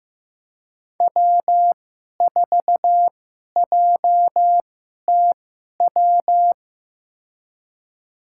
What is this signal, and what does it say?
Morse code, heard as a keyed tone, about 15 wpm, spaced as for 11 wpm, 700 Hz. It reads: W4JTW
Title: USA Call Signs - 5 Characters - 2x Character Spacing - 15wpm